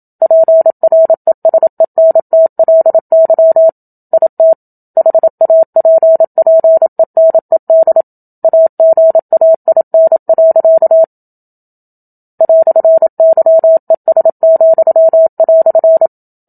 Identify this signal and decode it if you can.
Morse code, heard as a keyed tone, about 27 wpm, 650 Hz. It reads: PRESENTLY IT HAPPENED AGAIN. "YES,"
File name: Book Two / 01-Chapter 1 - 27 WPM